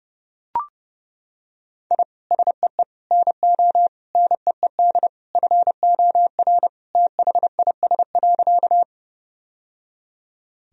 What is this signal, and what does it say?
Morse code, heard as a keyed tone, about 30 wpm, 700 Hz. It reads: I SEE NO NEED FOR THIS.
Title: Sentences from Top 1000 Words - Review Part 1 - 30wpm